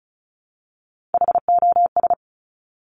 Morse code, heard as a keyed tone, about 36 wpm, 700 Hz. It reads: HOS
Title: Top Three Letter Combinations in Words 35wpm